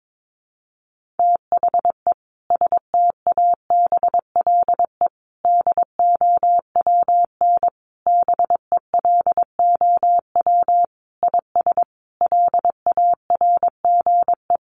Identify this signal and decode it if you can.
Morse code, heard as a keyed tone, about 22 wpm, 700 Hz. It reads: THE STABLE DOWN BELOW IS LARGE
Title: Sentences from Top 1500 Words 22wpm